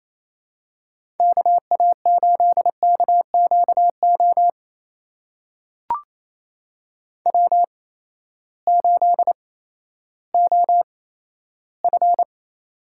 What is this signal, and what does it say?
Morse code, heard as a keyed tone, about 28 wpm, spaced as for 14 wpm, 700 Hz. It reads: KA8KQO W8OF
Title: USA Call Signs - 8x Character Spacing - 28wpm